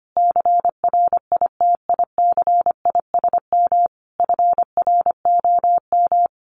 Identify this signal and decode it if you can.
Morse code, heard as a keyed tone, about 25 wpm, 700 Hz. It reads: CRITICISM FROM
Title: Sentences from Top 1300 Words 25wpm